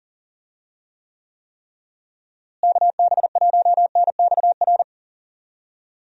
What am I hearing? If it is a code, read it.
Morse code, heard as a keyed tone, about 37 wpm, 700 Hz. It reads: KB1NXR
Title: USA Call Signs - 6 Characters - Repeated 3x - 40wpm